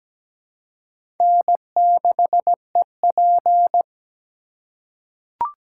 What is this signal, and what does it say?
Morse code, heard as a keyed tone, about 17 wpm, 700 Hz. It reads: N6EP
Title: USA Call Signs - 4 Characters - 2x Character Spacing - 17wpm